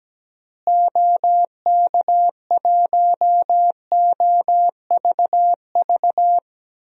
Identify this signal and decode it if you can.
Morse code, heard as a keyed tone, about 17 wpm, 700 Hz. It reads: OK1OVV
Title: DXCC-100 Call Signs - 6 Characters - Repeated 2x - 17wpm